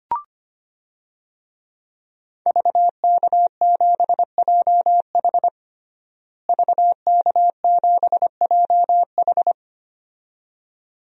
Morse code, heard as a keyed tone, about 25 wpm, 700 Hz. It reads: VK7JH VK7JH
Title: DXCC-100 Call Signs - 5 Characters - Repeated 2x - 25wpm